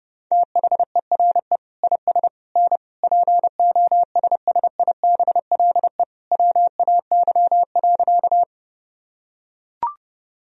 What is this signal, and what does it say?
Morse code, heard as a keyed tone, about 30 wpm, 700 Hz. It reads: THERE IS N POSSIBLE WAY.
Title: Sentences from Top 500 Words and Abbreviations 30wpm